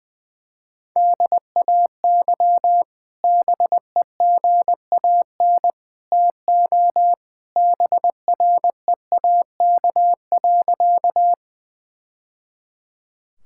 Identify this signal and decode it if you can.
Morse code, heard as a keyed tone, about 21 wpm, 700 Hz. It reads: DAY BEGAN TO BREAK.
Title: Sentences from Top 1000 Words - Review Part 6 - 20wpm